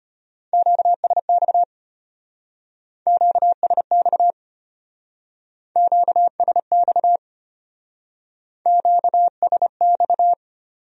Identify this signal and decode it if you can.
Morse code, heard as a keyed tone, about 31 wpm, 700 Hz. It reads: QSX QSX QSX QSX
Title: Speed-Racing Q-Codes - 25wpm